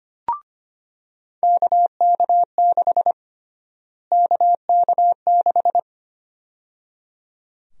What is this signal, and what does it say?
Morse code, heard as a keyed tone, about 25 wpm, 700 Hz. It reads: KK6 KK6
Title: USA Call Sign Prefixes - Repeated 2x - 25wpm